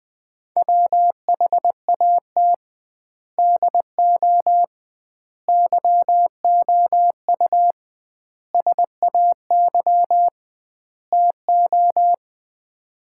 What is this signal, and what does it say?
Morse code, heard as a keyed tone, about 20 wpm, 700 Hz. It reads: WHAT DO YOU SAY TO
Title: 2x Word-Spacing - Sets of 5 Words - 20wpm